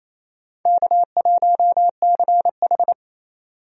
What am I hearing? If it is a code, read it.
Morse code, heard as a keyed tone, about 28 wpm, 700 Hz. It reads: K1CH
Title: USA Call Signs - 4 Characters - 28wpm